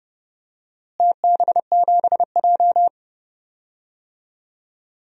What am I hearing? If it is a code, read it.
Morse code, heard as a keyed tone, about 30 wpm, 700 Hz. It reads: TB7J
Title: DXCC-100 Call Signs - 4 Characters - Repeated 3x - 30wpm